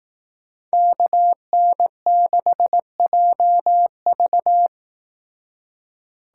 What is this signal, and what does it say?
Morse code, heard as a keyed tone, about 18 wpm, 700 Hz. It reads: KN6JV
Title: Speed Racing - USA Call Signs 15wpm